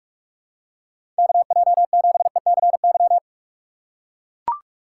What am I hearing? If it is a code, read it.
Morse code, heard as a keyed tone, about 41 wpm, 700 Hz. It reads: KJ7ECY
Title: USA Call Signs - 6 Characters - 2x Character Spacing - 45wpm